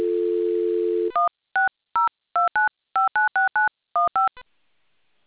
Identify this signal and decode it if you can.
DTMF, heard as telephone touch tones, tones of about 150 ms. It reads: 16029596915